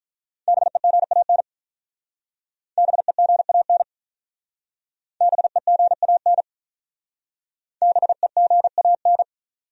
Morse code, heard as a keyed tone, about 45 wpm, 700 Hz. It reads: BEGAN BEGAN BEGAN BEGAN